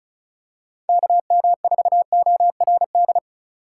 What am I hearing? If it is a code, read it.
Morse code, heard as a keyed tone, about 34 wpm, 690 Hz. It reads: KM4ORD
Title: USA Call Signs - 6 Characters - 4x Character Spacing - 35wpm